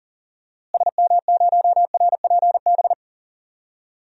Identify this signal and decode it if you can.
Morse code, heard as a keyed tone, about 37 wpm, 700 Hz. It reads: SM0RPB
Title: DXCC-100 Call Signs - 6 Characters - Repeated 3x - 40wpm